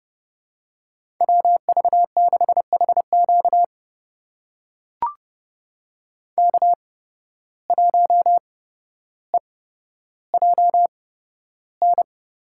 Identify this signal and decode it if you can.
Morse code, heard as a keyed tone, about 30 wpm, spaced as for 13 wpm, 700 Hz. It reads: WV6HQ K1EJN
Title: USA Call Signs - 5 Characters - 8x Character Spacing - 30wpm